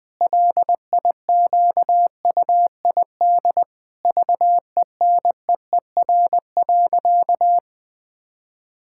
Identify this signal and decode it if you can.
Morse code, heard as a keyed tone, about 20 wpm, 700 Hz. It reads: LIQUID VENEER.
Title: Pangram Challenge v2 - 20wpm